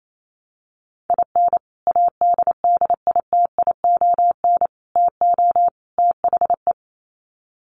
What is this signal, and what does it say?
Morse code, heard as a keyed tone, about 28 wpm, 700 Hz. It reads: IN ADDITION TO THE
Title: Sets of 4 Words 28wpm